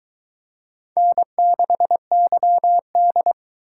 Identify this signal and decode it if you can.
Morse code, heard as a keyed tone, about 21 wpm, 700 Hz. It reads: N6YD